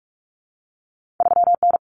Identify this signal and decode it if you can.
Morse code, heard as a keyed tone, about 46 wpm, 700 Hz. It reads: 3N
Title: Common Cut Numbers - 2-Digit 45wpm